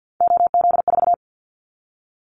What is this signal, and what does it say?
Morse code, heard as a keyed tone, about 50 wpm, 700 Hz. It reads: OZ4
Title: Callsign Prefixes for Top 50 DXCC - 50wpm